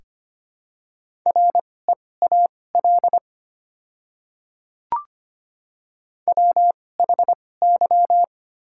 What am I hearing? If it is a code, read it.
Morse code, heard as a keyed tone, about 25 wpm, spaced as for 18 wpm, 700 Hz. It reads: REAL WHY
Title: 2x Character Spacing - Top 500 Words - 25wpm